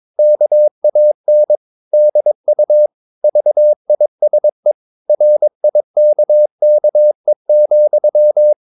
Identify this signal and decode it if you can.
Morse code, heard as a keyed tone, about 22 wpm, 600 Hz. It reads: KAN DU VISE RIKKE,
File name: Ferie QRS 22wpm